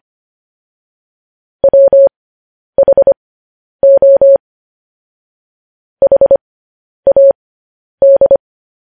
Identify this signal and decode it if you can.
Morse code, heard as a keyed tone, about 25 wpm, spaced as for 10 wpm, 550 Hz. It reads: WHO HAD
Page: news headlines in morse code at 10 wpm